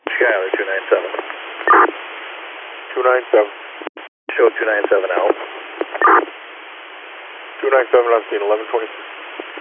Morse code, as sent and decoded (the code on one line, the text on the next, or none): none